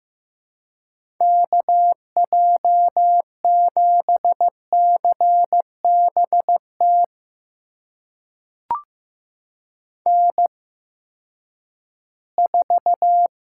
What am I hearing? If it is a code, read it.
Morse code, heard as a keyed tone, about 15 wpm, spaced as for 10 wpm, 700 Hz. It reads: KJ7CBT N4
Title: USA Call Signs - 8x Character Spacing - 15wpm